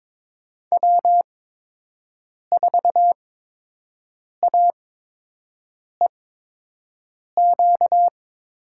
Morse code, heard as a keyed tone, about 22 wpm, spaced as for 6 wpm, 700 Hz. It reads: W4AEQ
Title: USA Call Signs - 8x Character Spacing - 22wpm